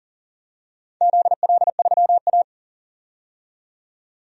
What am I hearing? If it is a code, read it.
Morse code, heard as a keyed tone, about 40 wpm, 700 Hz. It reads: ZL3A
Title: DXCC-100 Call Signs - 4 Characters - 40wpm